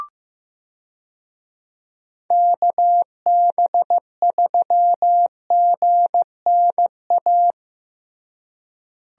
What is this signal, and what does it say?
Morse code, heard as a keyed tone, about 15 wpm, 700 Hz. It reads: KB3GNA
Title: USA Call Signs - 6 Characters - Repeated 2x - 15wpm